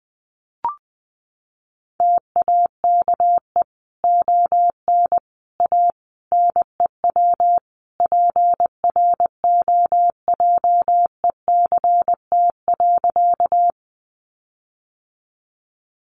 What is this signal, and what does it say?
Morse code, heard as a keyed tone, about 20 wpm, 700 Hz. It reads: TAKE ON A NEW PROJECT.
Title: Sentences from Top 1400 Words 20wpm